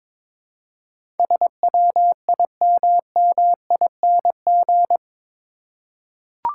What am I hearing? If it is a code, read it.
Morse code, heard as a keyed tone, about 21 wpm, 700 Hz. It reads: SWIMMING
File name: practice / 22wpm / Eight Letter Words - 22wpm